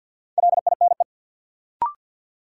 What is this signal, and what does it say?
Morse code, heard as a keyed tone, about 53 wpm, 700 Hz. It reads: LINE
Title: Sentences from Top 1000 Words - Review Part 1 - New Word Drill - 50wpm